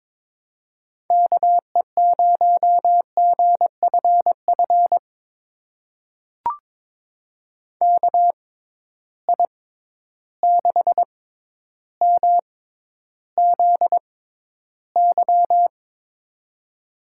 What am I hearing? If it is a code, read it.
Morse code, heard as a keyed tone, about 22 wpm, spaced as for 11 wpm, 700 Hz. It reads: KE0GFF KI6MZY
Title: USA Call Signs - 6 Characters - 6x Character Spacing - 22wpm